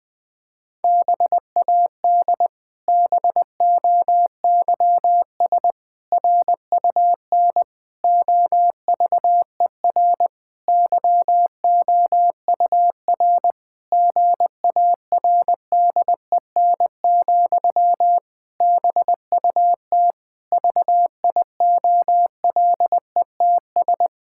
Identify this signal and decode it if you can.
Morse code, heard as a keyed tone, about 20 wpm, 700 Hz. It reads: BAD BOYS RUN OVER YOUR GARDEN, BUT VIOLETS
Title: Pangram Challenge v2 - 20wpm